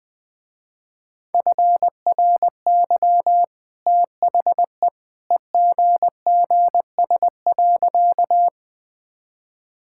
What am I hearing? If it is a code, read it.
Morse code, heard as a keyed tone, about 20 wpm, 700 Hz. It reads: FRY THE EGGS.
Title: Easy Sentences with Hints - 20wpm